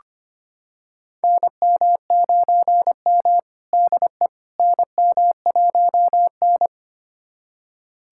Morse code, heard as a keyed tone, about 25 wpm, 700 Hz. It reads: NM9M DE NM1N
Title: USA Callsign DE Callsign - 4 Characters - 25wpm